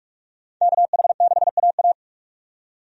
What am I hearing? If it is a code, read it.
Morse code, heard as a keyed tone, about 44 wpm, 700 Hz. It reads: KH6AA
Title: DXCC-100 Call Signs - 5 Characters - 45wpm